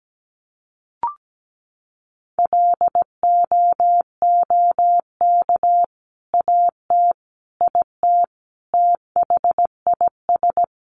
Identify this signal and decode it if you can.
Morse code, heard as a keyed tone, about 17 wpm, 700 Hz. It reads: LOOK AT IT THIS
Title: Sentences from Top 100 Words 17wpm